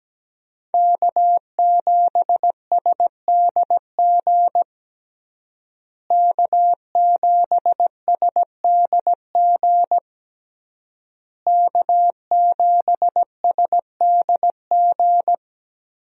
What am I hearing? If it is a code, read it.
Morse code, heard as a keyed tone, about 17 wpm, 700 Hz. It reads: K7SDG K7SDG K7SDG
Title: USA Call Signs - 5 Characters - Repeated 3x - 17wpm